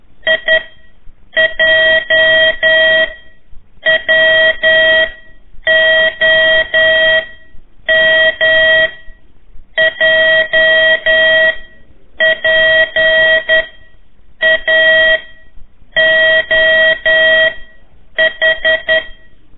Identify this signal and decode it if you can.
Morse code, heard as a keyed tone, about 11 wpm, spaced as for 7 wpm, 1900 Hz. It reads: IJWOM JPAOH
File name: 3 lezione CW 1 AIR